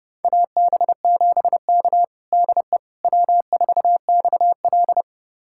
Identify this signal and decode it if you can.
Morse code, heard as a keyed tone, about 30 wpm, 700 Hz. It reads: AB7K DE W4XL
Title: USA Callsign DE Callsign - 4 Characters - 30wpm